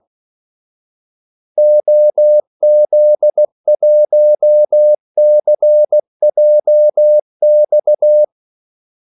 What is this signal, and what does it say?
Morse code, heard as a keyed tone, about 16 wpm, 600 Hz. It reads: OZ1CJX
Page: Kaldesignaler 16 wpm